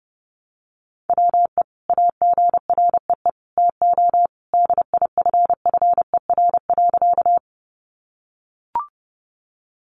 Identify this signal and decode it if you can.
Morse code, heard as a keyed tone, about 30 wpm, 700 Hz. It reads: WE AGREE TO DIFFER.
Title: Sentences from Top 1100 Words 30wpm